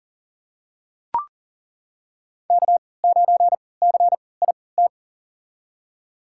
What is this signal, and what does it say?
Morse code, heard as a keyed tone, about 37 wpm, spaced as for 22 wpm, 700 Hz. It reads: K9CIT
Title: USA Call Signs - 5 Characters - 3x Character Spacing - 40wpm